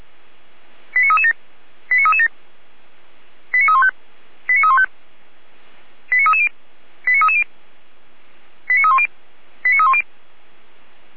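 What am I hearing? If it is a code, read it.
Five-tone selective call (ZVEI), heard as quick runs of five tones, tones of about 70 ms. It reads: 89337 89337 89315 89315 89330 89330 89310 89310
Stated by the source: ZVEI five-tone selective call received on SDR.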